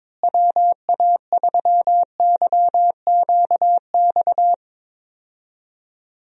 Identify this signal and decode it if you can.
Morse code, heard as a keyed tone, about 22 wpm, 700 Hz. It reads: WA3YQX
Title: USA Call Signs - 6 Characters - Repeated 2x - 22wpm